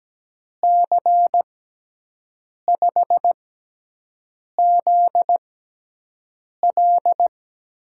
Morse code, heard as a keyed tone, about 17 wpm, spaced as for 6 wpm, 700 Hz. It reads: C5ZL
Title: USA Call Signs - 6 Characters - 6x Character Spacing - 17wpm